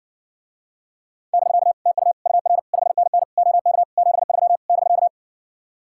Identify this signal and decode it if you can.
Morse code, heard as a keyed tone, about 70 wpm, 700 Hz. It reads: <BK> TU UR 5NN OK 73 <BK>